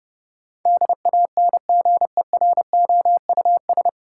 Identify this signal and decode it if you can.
Morse code, heard as a keyed tone, about 32 wpm, 700 Hz. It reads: DANGEROUS
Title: Sentences from Top 1200 Words - New Word Drill - 30wpm